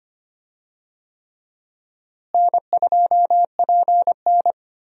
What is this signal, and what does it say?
Morse code, heard as a keyed tone, about 25 wpm, 700 Hz. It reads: N2PN
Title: USA Call Signs - 4 Characters - Repeated 3x - 25wpm